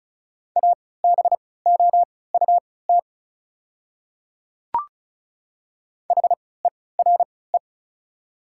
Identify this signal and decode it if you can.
Morse code, heard as a keyed tone, about 34 wpm, spaced as for 20 wpm, 700 Hz. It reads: ABOUT HERE
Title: 3x Character Spacing - Top 300 Words - 35wpm